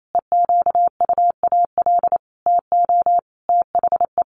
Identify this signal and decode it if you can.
Morse code, heard as a keyed tone, about 28 wpm, 700 Hz. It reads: EQUAL TO THE
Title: Sentences from Top 1400 Words 28wpm